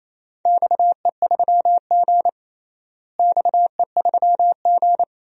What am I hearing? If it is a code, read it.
Morse code, heard as a keyed tone, about 28 wpm, 700 Hz. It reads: XE3G XE3G
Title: DXCC-100 Call Signs - 4 Characters - Repeated 2x - 28wpm